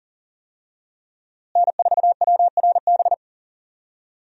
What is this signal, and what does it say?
Morse code, heard as a keyed tone, about 39 wpm, 700 Hz. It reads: N4WRB
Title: USA Call Signs - 5 Characters - Repeated 3x - 40wpm